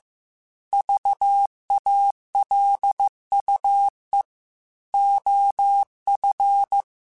Morse code, heard as a keyed tone, about 15 wpm, 800 Hz. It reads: VALUE OF